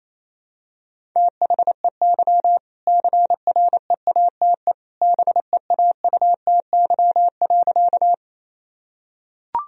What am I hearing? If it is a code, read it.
Morse code, heard as a keyed tone, about 28 wpm, 700 Hz. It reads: THEY CREATE BEAUTY.
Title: Sentences from Top 1000 Words - Review Part 3 - 28wpm